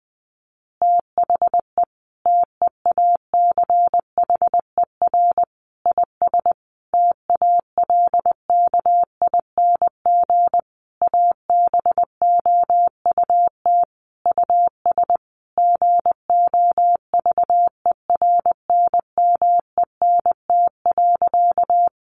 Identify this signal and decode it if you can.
Morse code, heard as a keyed tone, about 20 wpm, 700 Hz. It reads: THE TEACHER IS TALKING ABOUT US GOVERNMENT.
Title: Easy Phrases 20wpm